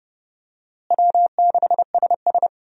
Morse code, heard as a keyed tone, about 30 wpm, 700 Hz. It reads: W6SS
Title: USA Call Signs - 4 Characters - Repeated 2x - 30wpm